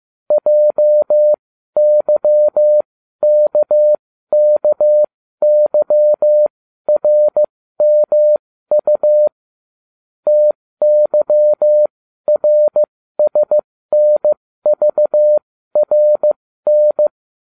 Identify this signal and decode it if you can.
Morse code, heard as a keyed tone, about 15 wpm, spaced as for 12 wpm, 600 Hz. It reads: JYKKYRMU TYRSNVRN